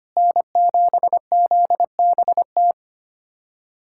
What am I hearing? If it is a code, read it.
Morse code, heard as a keyed tone, about 25 wpm, 700 Hz. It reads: N7ZBT
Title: USA Callsign DE Callsign 25wpm